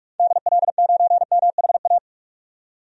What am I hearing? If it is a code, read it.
Morse code, heard as a keyed tone, about 44 wpm, 700 Hz. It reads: DL9MHA
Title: DXCC-100 Call Signs - 6 Characters - 45wpm